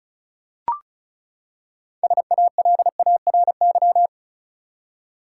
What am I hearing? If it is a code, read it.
Morse code, heard as a keyed tone, about 36 wpm, 700 Hz. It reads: SALARY
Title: Six Letter Words - 35wpm